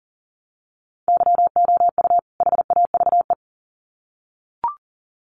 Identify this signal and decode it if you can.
Morse code, heard as a keyed tone, about 40 wpm, 700 Hz. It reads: YOU HAVE
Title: Sets of 2 Words 40wpm